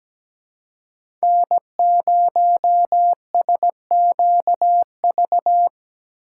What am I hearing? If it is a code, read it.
Morse code, heard as a keyed tone, about 17 wpm, 700 Hz. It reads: N0SQV
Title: USA Call Signs - 5 Characters - 4x Character Spacing - 17wpm